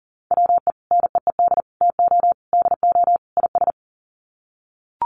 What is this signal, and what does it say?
Morse code, heard as a keyed tone, about 40 wpm, 700 Hz. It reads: WE NEED TO DO IS
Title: Sets of 5 Words Encore2 40wpm